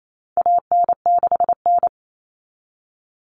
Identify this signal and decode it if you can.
Morse code, heard as a keyed tone, about 28 wpm, 700 Hz. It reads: AN6N
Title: Common Cut Numbers - 4-Digit 28wpm